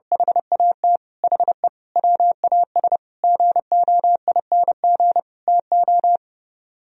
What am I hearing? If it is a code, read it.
Morse code, heard as a keyed tone, about 29 wpm, 700 Hz. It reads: HAT HE WAS GOING TO